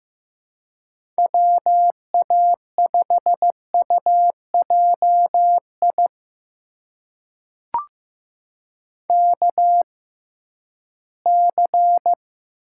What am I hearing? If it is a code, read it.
Morse code, heard as a keyed tone, about 15 wpm, spaced as for 11 wpm, 700 Hz. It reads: WA5UJI KC
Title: USA Call Signs - 6x Character Spacing - 15wpm